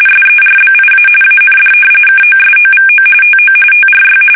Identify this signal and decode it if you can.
RTTY, about 45.45 baud. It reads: SERBIA AND KOSOVO TO TRY
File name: fsk850 / usnavy5